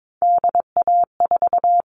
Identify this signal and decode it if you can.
Morse code, heard as a keyed tone, about 22 wpm, 700 Hz. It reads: DA4
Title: All Cut Numbers - 3-Digit 22wpm